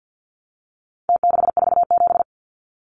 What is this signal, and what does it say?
Morse code, heard as a keyed tone, about 49 wpm, 700 Hz. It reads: T647
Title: Numbers - Contest Style - 4-Digit 50wpm